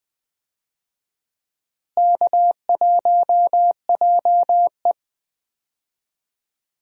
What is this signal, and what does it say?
Morse code, heard as a keyed tone, about 20 wpm, 700 Hz. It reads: K1JE
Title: USA Call Signs - 4 Characters - Repeated 3x - 20wpm